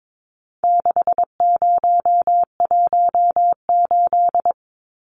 Morse code, heard as a keyed tone, about 22 wpm, 700 Hz. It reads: 6018
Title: Numbers - 4-Digit 22wpm